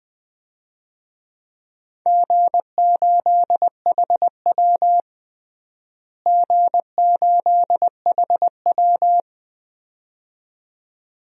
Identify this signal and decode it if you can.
Morse code, heard as a keyed tone, about 21 wpm, 700 Hz. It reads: G8HW G8HW
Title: DXCC-100 Call Signs - 4 Characters - Repeated 2x - 20wpm